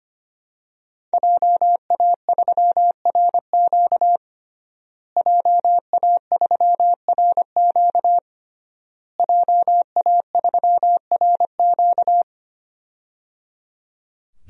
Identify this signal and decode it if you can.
Morse code, heard as a keyed tone, about 25 wpm, 700 Hz. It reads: JA3RQ JA3RQ JA3RQ
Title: DXCC-100 Call Signs - 5 Characters - Repeated 3x - 25wpm